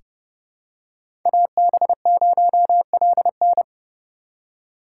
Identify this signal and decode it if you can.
Morse code, heard as a keyed tone, about 30 wpm, 700 Hz. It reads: AB0LN
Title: USA Call Signs - 5 Characters - 3x Character Spacing - 30wpm